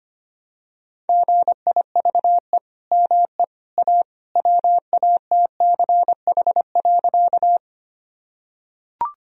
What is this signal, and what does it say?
Morse code, heard as a keyed tone, about 25 wpm, 700 Hz. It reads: GIVE ME A WATCH.
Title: Sentences from Top 1000 Words - Review Part 1 - 25wpm